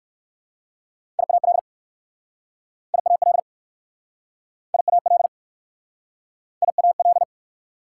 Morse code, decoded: IAL IAL IAL IAL

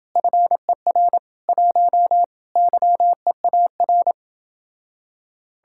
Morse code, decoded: FER 1 YEAR

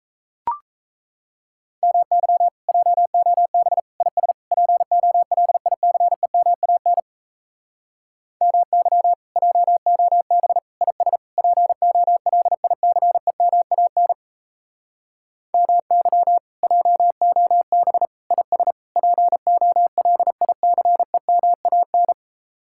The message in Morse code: MY JOB IS POLICEMAN MY JOB IS POLICEMAN MY JOB IS POLICEMAN

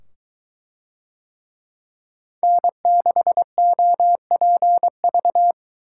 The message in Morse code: N6OPV

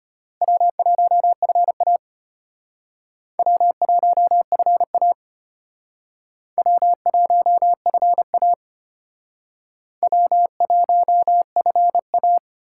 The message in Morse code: W1FA W1FA W1FA W1FA

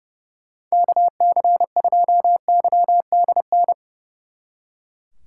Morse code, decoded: KC2YDN

